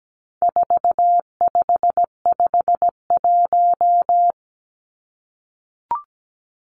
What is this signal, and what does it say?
Morse code, heard as a keyed tone, about 17 wpm, 700 Hz. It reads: 4551
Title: Numbers - 4-Digit 17wpm